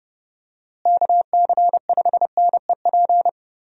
Morse code, heard as a keyed tone, about 30 wpm, 700 Hz. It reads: KC5NEP
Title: USA Call Signs - 6 Characters - Repeated 2x - 30wpm